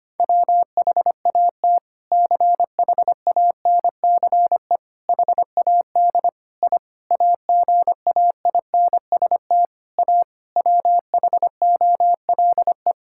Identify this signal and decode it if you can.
Morse code, heard as a keyed tone, about 25 wpm, 700 Hz. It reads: WHAT CHANCE HAD I AGAINST A WHOLE
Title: Sentences from Top 1000 Words - Review Part 6 - 25wpm